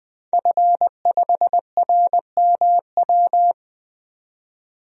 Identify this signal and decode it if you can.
Morse code, heard as a keyed tone, about 21 wpm, 700 Hz. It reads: F5RMW
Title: DXCC-100 Call Signs - 5 Characters - 20wpm